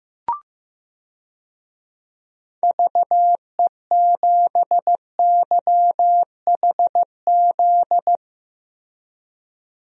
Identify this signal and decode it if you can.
Morse code, heard as a keyed tone, about 15 wpm, 700 Hz. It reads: VE7YHZ
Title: DXCC-100 Call Signs - 6 Characters - Repeated 3x - 15wpm